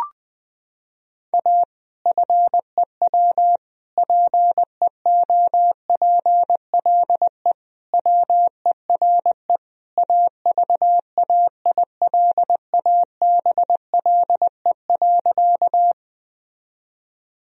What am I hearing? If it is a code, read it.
Morse code, heard as a keyed tone, about 20 wpm, 700 Hz. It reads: A FEW PEOPLE WERE AVAILABLE.
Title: Sentences from Top 1000 Words - Review Part 3 - 20wpm